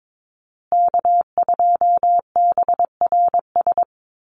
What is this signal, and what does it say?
Morse code, heard as a keyed tone, about 22 wpm, 700 Hz. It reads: K2BRS
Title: USA Call Signs Repeated 2x 22wpm